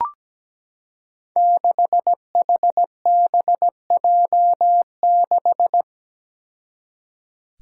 Morse code, heard as a keyed tone, about 17 wpm, 700 Hz. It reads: 6HBJ6
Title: Five Challenging Letter-Number Combinations 17wpm